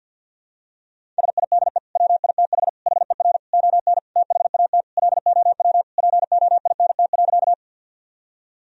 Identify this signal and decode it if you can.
Morse code, heard as a keyed tone, about 52 wpm, 700 Hz. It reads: SIDE WITH HER ON THAT LOW POINT.
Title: Sentences from Top 1000 Words - Review Part 1 - 50wpm